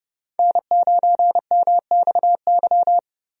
Morse code, heard as a keyed tone, about 30 wpm, 700 Hz. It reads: N9MXY
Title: USA Call Signs - 5 Characters - Repeated 3x - 30wpm